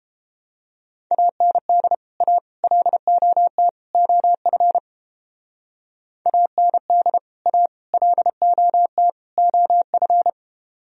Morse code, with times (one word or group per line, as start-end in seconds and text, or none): 1.1-1.9 AND
2.2-2.4 A
2.6-3.7 LOT
3.9-4.8 OF
6.3-7.2 AND
7.5-7.7 A
7.9-9.1 LOT
9.4-10.3 OF